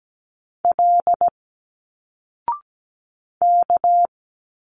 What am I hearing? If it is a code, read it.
Morse code, heard as a keyed tone, about 17 wpm, 700 Hz. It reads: L K
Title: Single Letters 17wpm